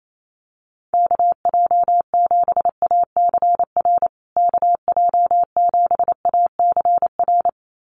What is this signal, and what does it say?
Morse code, heard as a keyed tone, about 28 wpm, 700 Hz. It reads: KJ7ACR KJ7ACR
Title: USA Call Signs Repeated 2x 28wpm